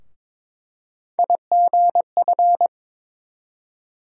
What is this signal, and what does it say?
Morse code, heard as a keyed tone, about 22 wpm, 700 Hz. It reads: IGF
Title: USA Call Sign Postfixes - 3 Letters 22wpm